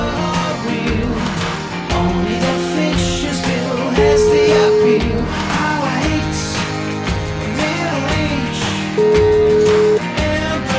Ringback tone (Europe, 425 Hz), with noise.